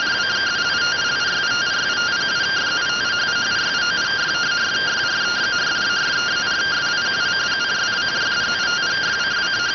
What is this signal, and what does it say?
Two-tone teleprinter signal, about 170 Hz shift.